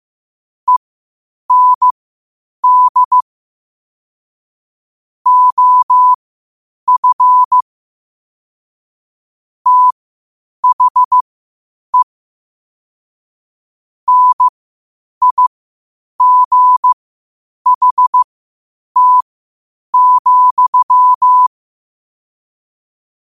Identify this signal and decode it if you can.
Morse code, heard as a keyed tone, about 15 wpm, spaced as for 8 wpm, 1000 Hz. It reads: END OF THE NIGHT,